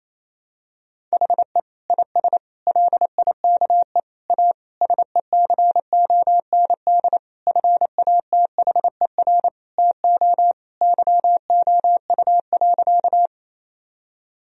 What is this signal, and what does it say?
Morse code, heard as a keyed tone, about 28 wpm, 700 Hz. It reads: HE IS LIKE A SECOND FATHER TO YOU.
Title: Sentences from Top 1000 Words - Review Part 1 - 28wpm